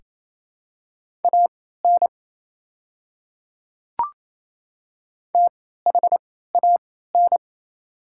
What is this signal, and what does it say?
Morse code, heard as a keyed tone, about 27 wpm, spaced as for 16 wpm, 690 Hz. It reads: AN THAN
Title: 3x Character Spacing - Top 200 Words - 28wpm